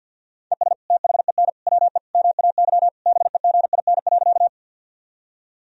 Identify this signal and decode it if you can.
Morse code, heard as a keyed tone, about 50 wpm, 700 Hz. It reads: ES THEN WE MAY BEGIN.
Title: Sentences from Top 300 Words and Abbreviations 50wpm